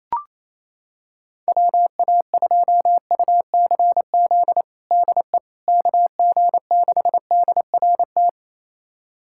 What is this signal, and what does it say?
Morse code, heard as a keyed tone, about 28 wpm, 700 Hz. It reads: WA2UCZ DE KG6DRT
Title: USA Callsign DE Callsign - 6 Characters - 28wpm